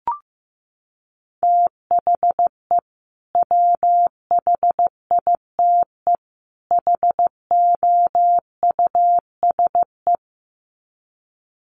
Morse code, heard as a keyed tone, about 14 wpm, 700 Hz. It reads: THE WHITE HOUSE
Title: Sets of 3 Words 15wpm